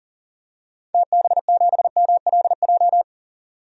Morse code, heard as a keyed tone, about 40 wpm, 700 Hz. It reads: TB7MLJ
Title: DXCC-100 Call Signs - 6 Characters - 40wpm